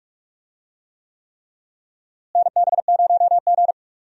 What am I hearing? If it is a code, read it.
Morse code, heard as a keyed tone, about 45 wpm, 700 Hz. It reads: NB0G
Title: USA Call Signs - 4 Characters - Repeated 3x - 45wpm